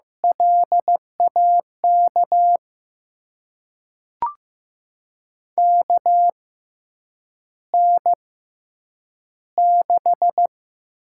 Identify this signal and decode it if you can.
Morse code, heard as a keyed tone, about 15 wpm, spaced as for 8 wpm, 700 Hz. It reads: LAK KN6
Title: USA Call Signs - 5 Characters - 6x Character Spacing - 15wpm